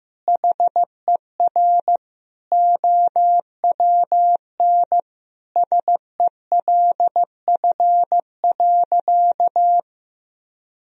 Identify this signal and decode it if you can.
Morse code, heard as a keyed tone, about 15 wpm, 700 Hz. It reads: HER OWN SELF.
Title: Sentences from Top 1000 Words - Review Part 1 - 15wpm